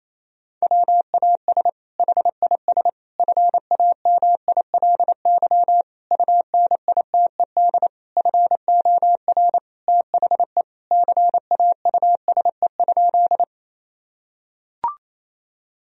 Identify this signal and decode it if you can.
Morse code, heard as a keyed tone, about 28 wpm, 700 Hz. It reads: WAS HIS FAMILY UNITED FOR THE CAUSE?
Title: Sentences from Top 1000 Words - Review Part 4 - 28wpm